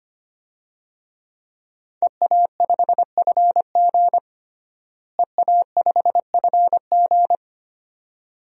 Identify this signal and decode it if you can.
Morse code, heard as a keyed tone, about 25 wpm, 700 Hz. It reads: EA5FG EA5FG
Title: DXCC-100 Call Signs - 5 Characters - Repeated 2x - 25wpm